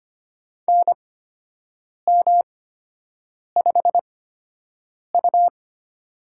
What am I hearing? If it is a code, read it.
Morse code, heard as a keyed tone, about 25 wpm, spaced as for 7 wpm, 700 Hz. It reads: NM5U
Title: USA Call Signs - 4 Characters - 8x Character Spacing - 25wpm